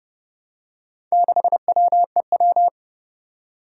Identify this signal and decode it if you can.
Morse code, heard as a keyed tone, about 30 wpm, 700 Hz. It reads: 6WEW
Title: Four Challenging Letter-Number Combinations 30wpm